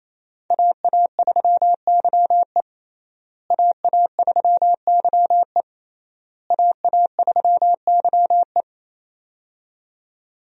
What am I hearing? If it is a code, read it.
Morse code, heard as a keyed tone, about 28 wpm, 700 Hz. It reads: AA3YE AA3YE AA3YE
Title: USA Call Signs - 5 Characters - Repeated 3x - 28wpm